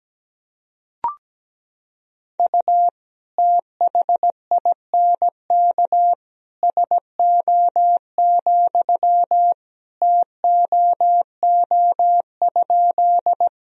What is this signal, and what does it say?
Morse code, heard as a keyed tone, about 17 wpm, 700 Hz. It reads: U THINK SO, TOO?
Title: Sentences from Top 200 Words and Abbreviations 17wpm